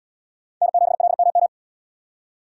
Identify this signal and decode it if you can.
Morse code, heard as a keyed tone, about 68 wpm, 700 Hz. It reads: N7LMM